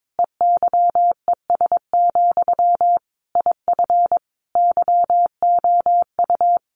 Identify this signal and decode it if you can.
Morse code, heard as a keyed tone, about 22 wpm, 700 Hz. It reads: EYES, IF YOU
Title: A Sound Of Thunder 22wpm